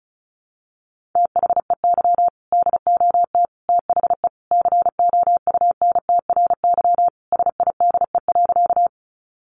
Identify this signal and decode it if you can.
Morse code, heard as a keyed tone, about 34 wpm, 700 Hz. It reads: THEY DOT THE COUNTRY SIDE.
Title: Sentences from Top 1500 Words 35wpm